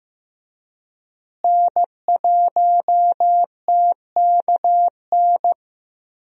Morse code, heard as a keyed tone, about 15 wpm, 700 Hz. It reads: N1TKN
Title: USA Call Signs - 5 Characters - Repeated 2x - 15wpm